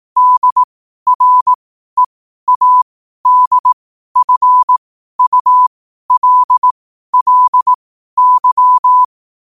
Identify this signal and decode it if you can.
Morse code, heard as a keyed tone, about 18 wpm, spaced as for 12 wpm, 1000 Hz. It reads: DREADFULLY
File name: qotdfiles / 12wpm